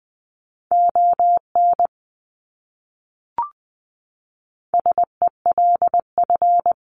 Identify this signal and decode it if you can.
Morse code, heard as a keyed tone, about 20 wpm, 700 Hz. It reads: ON SELF
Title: Top 300 Words Repeated 20wpm